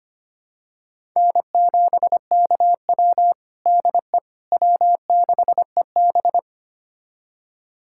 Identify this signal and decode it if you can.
Morse code, heard as a keyed tone, about 25 wpm, 700 Hz. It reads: N7KW DE W6EB
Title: USA Callsign DE Callsign - 4 Characters - 25wpm